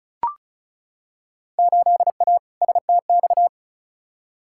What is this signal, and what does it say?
Morse code, heard as a keyed tone, about 36 wpm, 700 Hz. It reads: 8A STX
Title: Field Day Exchanges Encore - 35wpm